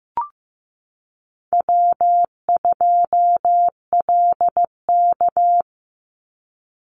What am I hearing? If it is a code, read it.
Morse code, heard as a keyed tone, about 15 wpm, 700 Hz. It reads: W2LK
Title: Easy Call Signs - 15wpm